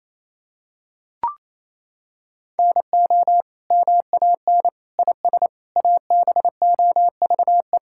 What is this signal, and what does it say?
Morse code, heard as a keyed tone, about 28 wpm, 700 Hz. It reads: NO MAN IS ABOVE
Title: Sentences from Top 1000 Words - Review Part 5 - 28wpm